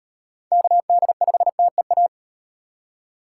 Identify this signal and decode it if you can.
Morse code, heard as a keyed tone, about 34 wpm, 700 Hz. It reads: KD5TEA